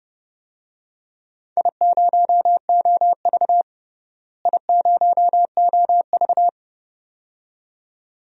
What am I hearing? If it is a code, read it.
Morse code, heard as a keyed tone, about 31 wpm, 700 Hz. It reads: I0OV I0OV
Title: DXCC-100 Call Signs - 4 Characters - Repeated 2x - 30wpm